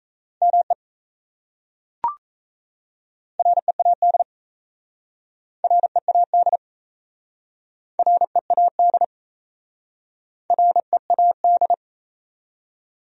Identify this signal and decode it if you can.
Morse code, heard as a keyed tone, about 37 wpm, 700 Hz. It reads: ME READ READ READ READ